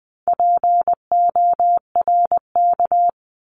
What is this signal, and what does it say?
Morse code, heard as a keyed tone, about 20 wpm, 700 Hz. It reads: PORK
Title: 260 Lesson 42e - Words 20-20wpm